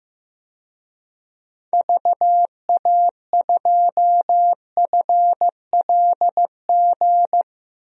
Morse code, heard as a keyed tone, about 15 wpm, 700 Hz. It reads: VA2FLG
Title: DXCC-100 Call Signs - 6 Characters - 15wpm